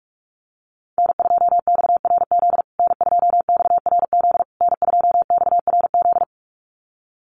Morse code, heard as a keyed tone, about 44 wpm, 700 Hz. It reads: N2XRZ N2XRZ N2XRZ
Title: USA Call Signs Repeated 3x 45wpm